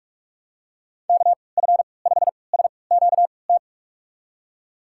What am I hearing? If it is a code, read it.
Morse code, heard as a keyed tone, about 46 wpm, spaced as for 26 wpm, 700 Hz. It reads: KF5SQT
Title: USA Call Signs - 3x Character Spacing - 45wpm